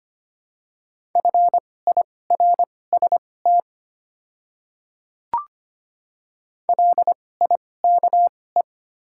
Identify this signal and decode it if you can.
Morse code, heard as a keyed tone, about 25 wpm, spaced as for 18 wpm, 690 Hz. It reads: FIRST LIKE
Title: 2x Character Spacing - Top 100 Words - 25wpm